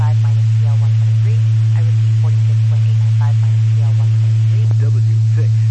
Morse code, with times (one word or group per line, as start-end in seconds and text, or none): none